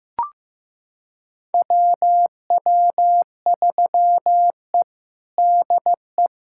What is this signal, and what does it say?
Morse code, heard as a keyed tone, about 15 wpm, 700 Hz. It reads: WW3E DE
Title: USA Callsign DE Callsign - 4 Characters - 15wpm